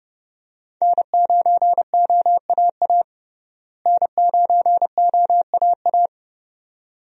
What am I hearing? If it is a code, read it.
Morse code, heard as a keyed tone, about 31 wpm, 700 Hz. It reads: N9OAA N9OAA
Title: USA Call Signs - 5 Characters - Repeated 2x - 30wpm